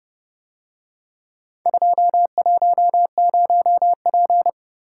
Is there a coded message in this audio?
Yes: Morse code from 1.7 to 4.5 s, heard as a keyed tone, about 30 wpm, 700 Hz. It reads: 210P